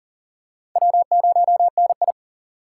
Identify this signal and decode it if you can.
Morse code, heard as a keyed tone, about 40 wpm, 700 Hz. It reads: W0NI
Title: USA Call Signs - 4 Characters - 4x Character Spacing - 40wpm